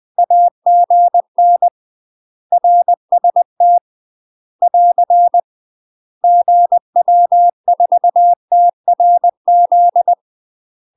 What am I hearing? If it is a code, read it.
Morse code, heard as a keyed tone, about 20 wpm, 700 Hz. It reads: AGN RST <AR> GW4TRZ
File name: UK MASTER QSOs TRAINING / 20WPM / UK MASTER QSOs 0072